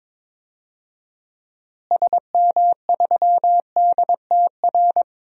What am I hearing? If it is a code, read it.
Morse code, heard as a keyed tone, about 22 wpm, 700 Hz. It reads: SM3DTR
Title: DXCC-100 Call Signs - 6 Characters - Repeated 3x - 22wpm